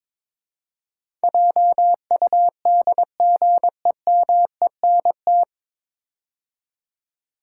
Morse code, heard as a keyed tone, about 22 wpm, 700 Hz. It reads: JUDGEMENT
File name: 9 Letter Words - First 3 Letters Given - 22wpm